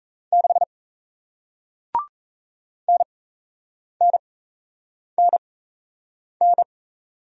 Morse code, heard as a keyed tone, about 38 wpm, spaced as for 8 wpm, 700 Hz. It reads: 6 NNNN